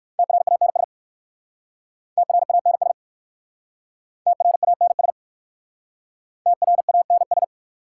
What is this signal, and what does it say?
Morse code, heard as a keyed tone, about 57 wpm, 700 Hz. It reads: TRANS TRANS TRANS TRANS